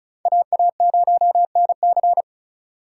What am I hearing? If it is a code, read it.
Morse code, heard as a keyed tone, about 43 wpm, 700 Hz. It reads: AA0NC
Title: Speed Racing - USA Call Signs 35wpm